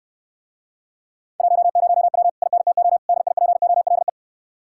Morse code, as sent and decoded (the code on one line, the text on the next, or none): .---- ----- .-- / .. -. - --- / -.. .. .--. --- .-.. .
10W INTO DIPOLE